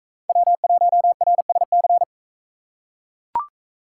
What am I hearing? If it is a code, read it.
Morse code, heard as a keyed tone, about 42 wpm, 700 Hz. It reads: W1RSC